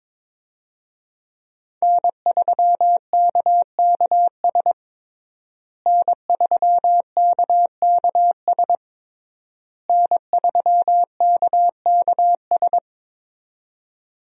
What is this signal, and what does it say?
Morse code, heard as a keyed tone, about 22 wpm, 700 Hz. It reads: N3KKS N3KKS N3KKS
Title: USA Call Signs - 5 Characters - Repeated 3x - 22wpm